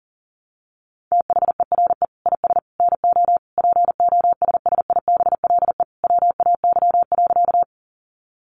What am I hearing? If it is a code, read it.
Morse code, heard as a keyed tone, about 40 wpm, 700 Hz. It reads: THERE IS NO POSSIBLE WAY.
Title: Sentences from Top 500 Words 40wpm